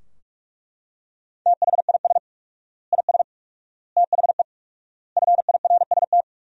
Morse code, heard as a keyed tone, about 47 wpm, 700 Hz. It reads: THIS IS THE FIRST
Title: 4x Word-Spacing - Sentences from Top 200 Words - 45wpm